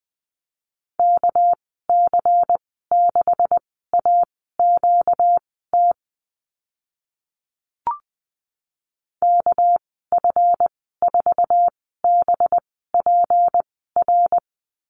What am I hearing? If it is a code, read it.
Morse code, heard as a keyed tone, about 20 wpm, spaced as for 14 wpm, 700 Hz. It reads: KC6AQT KF4BPR
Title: 245 Lesson 39g - Letter z - Call Signs 20wpm